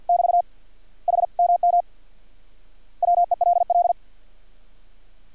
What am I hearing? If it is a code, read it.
Morse code, heard as a keyed tone, about 50 wpm, 700 Hz. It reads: <BT> HMM WELL
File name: archived / CRSnet